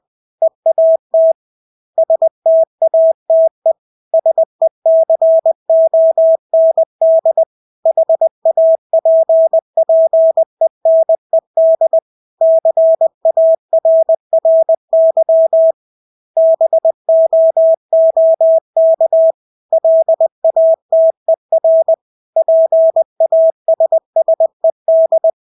Morse code, heard as a keyed tone, about 20 wpm, 650 Hz. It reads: EAT STATE SECOND HAPPENED CARRY BOOK LATER PASSED
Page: Engelske ord 20 wpm